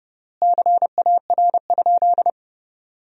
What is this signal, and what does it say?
Morse code, heard as a keyed tone, about 31 wpm, 700 Hz. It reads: CAR?
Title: 3x Word-Spacing - Sentences from Top 300 Words - 30wpm